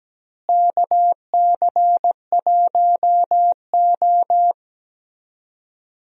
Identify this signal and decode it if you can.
Morse code, heard as a keyed tone, about 17 wpm, 700 Hz. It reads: KC1O
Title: USA Call Signs - 4 Characters - Repeated 2x - 17wpm